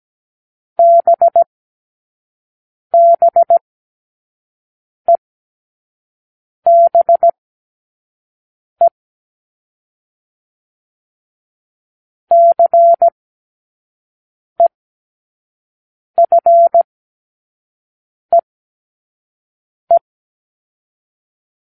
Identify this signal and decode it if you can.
Morse code, heard as a keyed tone, about 17 wpm, spaced as for 5 wpm, 680 Hz. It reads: BBEBE CEFEE
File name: k7qo / Lesson9(A-F)